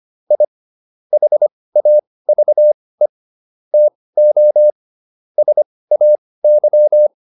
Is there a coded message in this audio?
Yes: Morse code from 0.3 to 7.1 s, heard as a keyed tone, about 25 wpm, spaced as for 18 wpm, 600 Hz. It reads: I HAVE TO SAY